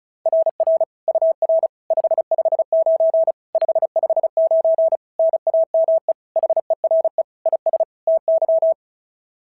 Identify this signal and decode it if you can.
Morse code, heard as a keyed tone, about 36 wpm, 690 Hz. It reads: RR UR 559 559 NAME HERE IS TY
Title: SOTA QSO Practice 35wpm